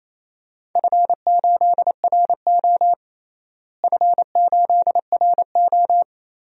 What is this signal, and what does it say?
Morse code, heard as a keyed tone, about 28 wpm, 700 Hz. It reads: F8RO F8RO
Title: DXCC-100 Call Signs - 4 Characters - Repeated 3x - 28wpm